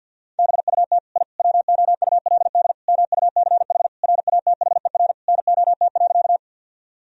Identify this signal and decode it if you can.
Morse code, heard as a keyed tone, about 50 wpm, 700 Hz. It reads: BUT I WOULD MUCH RATHER NOT.
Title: Sentences from Top 1000 Words - Review Part 4 - 50wpm